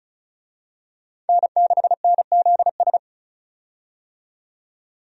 Morse code, heard as a keyed tone, about 36 wpm, 700 Hz. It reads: N6NZS
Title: USA Call Signs - 5 Characters - 35wpm